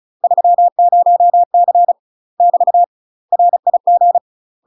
Morse code, heard as a keyed tone, about 35 wpm, 700 Hz. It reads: 30C <BT> RIG
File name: UK MASTER QSOs TRAINING / 35WPM / UK MASTER QSOs 0003